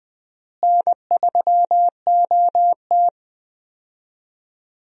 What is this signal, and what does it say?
Morse code, heard as a keyed tone, about 20 wpm, 700 Hz. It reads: N3OT
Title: USA Call Signs - 4 Characters - Repeated 3x - 20wpm